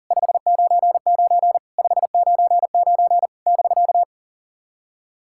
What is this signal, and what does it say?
Morse code, heard as a keyed tone, about 40 wpm, 690 Hz. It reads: 599 599 <BK>
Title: SOTA QSO Practice 40wpm